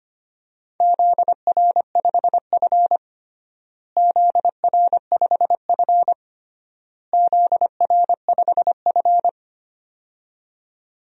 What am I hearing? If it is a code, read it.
Morse code, heard as a keyed tone, about 25 wpm, 700 Hz. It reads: ZR5F ZR5F ZR5F
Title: DXCC-100 Call Signs - 4 Characters - Repeated 3x - 25wpm